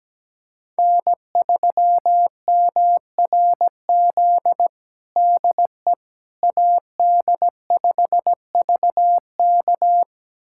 Morse code, heard as a keyed tone, about 17 wpm, 700 Hz. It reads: N3MRZ DE AD5VK
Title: USA Callsign DE Callsign - 5 Characters - 17wpm